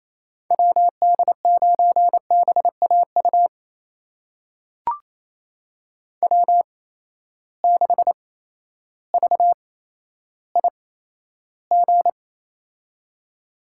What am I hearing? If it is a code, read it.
Morse code, heard as a keyed tone, about 28 wpm, spaced as for 13 wpm, 700 Hz. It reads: WD9BAU W6VIG
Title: USA Call Signs - 8x Character Spacing - 28wpm